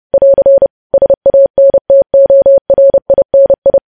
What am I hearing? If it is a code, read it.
Morse code, heard as a keyed tone, about 30 wpm, 550 Hz. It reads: <AR> SANTORINI
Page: News Headlines in Morse Code at 30 WPM